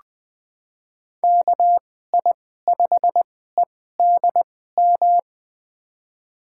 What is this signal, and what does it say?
Morse code, heard as a keyed tone, about 20 wpm, spaced as for 15 wpm, 700 Hz. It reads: KI5EDM
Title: USA Call Signs - 6 Characters - 2x Character Spacing - 20wpm